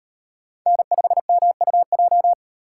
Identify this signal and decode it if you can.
Morse code, heard as a keyed tone, about 35 wpm, 700 Hz. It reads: N5MUJ